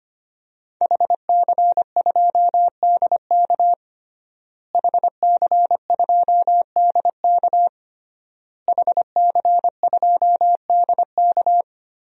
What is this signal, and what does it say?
Morse code, heard as a keyed tone, about 25 wpm, 700 Hz. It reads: HC2DK HC2DK HC2DK
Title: DXCC-100 Call Signs - 5 Characters - Repeated 3x - 25wpm